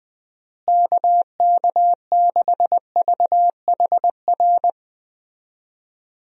Morse code, heard as a keyed tone, about 20 wpm, 700 Hz. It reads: KK6VHR
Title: USA Call Signs - 6 Characters - Repeated 3x - 20wpm